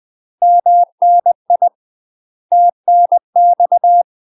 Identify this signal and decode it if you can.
Morse code, heard as a keyed tone, about 20 wpm, 700 Hz. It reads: MNI TNX
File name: UK MASTER QSOs TRAINING / 20WPM / UK MASTER QSOs 0071